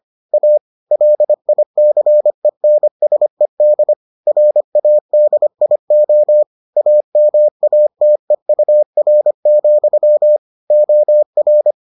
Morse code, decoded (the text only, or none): A LICENSED RADIO AMATEUR, OR